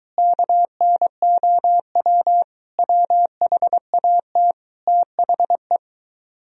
Morse code, heard as a keyed tone, about 21 wpm, 700 Hz. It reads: KNOW WHAT THE